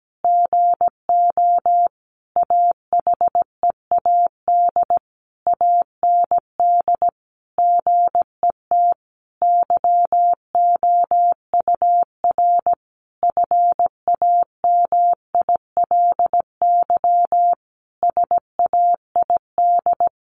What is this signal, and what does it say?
Morse code, heard as a keyed tone, about 17 wpm, 700 Hz. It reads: GO AHEAD AND GET YOUR FAMILY SAID
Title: Sentences from Top 1100 Words 17wpm